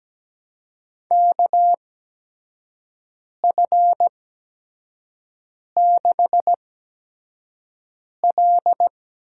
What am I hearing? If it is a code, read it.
Morse code, heard as a keyed tone, about 17 wpm, spaced as for 5 wpm, 700 Hz. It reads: KF6L
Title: USA Call Signs - 6 Characters - 8x Character Spacing - 17wpm